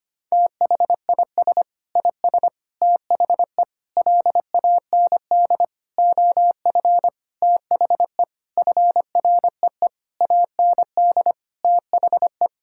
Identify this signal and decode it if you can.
Morse code, heard as a keyed tone, about 25 wpm, 700 Hz. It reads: THIS IS THE LAND OF THE FREE AND THE
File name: practice / 25wpm / Sentences with 9-Word Phrases - 25wpm